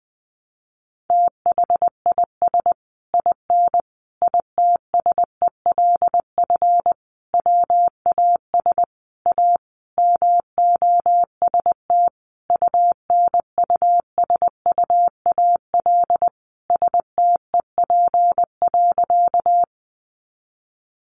Morse code, heard as a keyed tone, about 20 wpm, 700 Hz. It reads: THIS IN ITSELF WAS A MOST UNUSUAL STEP.
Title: Sentences from Top 1400 Words 20wpm